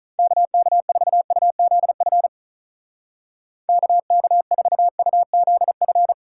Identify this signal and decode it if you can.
Morse code, heard as a keyed tone, about 44 wpm, 700 Hz. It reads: KK4UZF KK4UZF